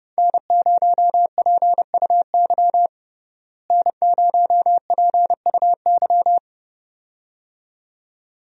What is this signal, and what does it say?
Morse code, heard as a keyed tone, about 30 wpm, 700 Hz. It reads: N0PUY N0PUY
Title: USA Call Signs - 5 Characters - Repeated 2x - 30wpm